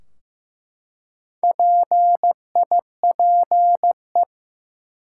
Morse code, heard as a keyed tone, about 15 wpm, 700 Hz. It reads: PIPE
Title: Four Letter Words - 15wpm